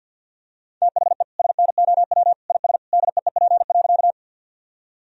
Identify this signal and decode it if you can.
Morse code, heard as a keyed tone, about 50 wpm, 700 Hz. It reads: THE SNOW IS DEEP.
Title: Sentences from Top 1000 Words - Review Part 3 - 50wpm